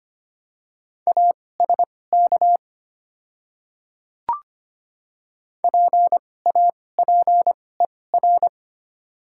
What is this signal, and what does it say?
Morse code, heard as a keyed tone, about 25 wpm, spaced as for 18 wpm, 700 Hz. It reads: ASK PAPER
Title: 2x Character Spacing - Top 300 Words - 25wpm